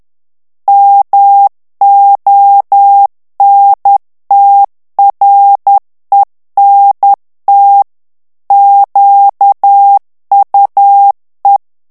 Morse code, heard as a keyed tone, about 10 wpm, 800 Hz. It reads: MONTRENT QUE
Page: Français - texte en clair 1 - vitesse : 10 mots minute